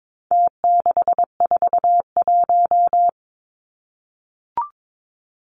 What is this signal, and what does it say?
Morse code, heard as a keyed tone, about 22 wpm, 700 Hz. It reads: T641
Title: Numbers - Contest Style - 4-Digit 22wpm